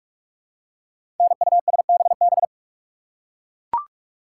Morse code, heard as a keyed tone, about 44 wpm, 700 Hz. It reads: NUSBB
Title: Code Groups - 5 Letters - 45wpm